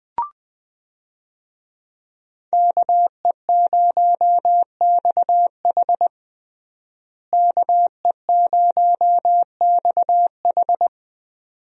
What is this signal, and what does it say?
Morse code, heard as a keyed tone, about 20 wpm, 700 Hz. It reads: KE0XH KE0XH
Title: USA Call Signs - 5 Characters - Repeated 2x - 20wpm